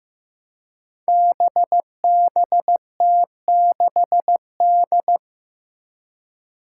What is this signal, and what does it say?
Morse code, heard as a keyed tone, about 15 wpm, 700 Hz. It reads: BBT6D
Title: Five Challenging Letter-Number Combinations 15wpm